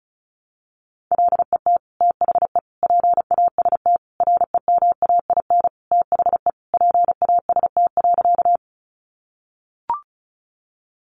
Morse code, heard as a keyed tone, about 34 wpm, 700 Hz. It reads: LET THE PAST REMAIN THE PAST.
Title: Sentences from Top 1200 Words 35wpm